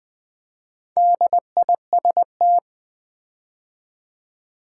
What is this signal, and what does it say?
Morse code, heard as a keyed tone, about 20 wpm, 700 Hz. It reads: DIST
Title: Build A Word - 4 Letters At A Time - 8-Letter Words - 20wpm